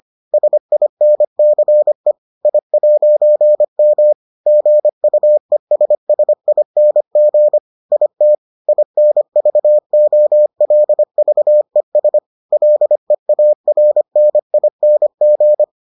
Morse code, heard as a keyed tone, about 25 wpm, 600 Hz. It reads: SINCE I'M GUESSING IT INVOLVES LEARNING